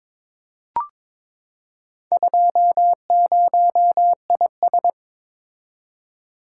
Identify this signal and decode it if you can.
Morse code, heard as a keyed tone, about 23 wpm, 700 Hz. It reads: 20IS
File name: practice / 22wpm / Four Letter-Number Combinations - 22wpm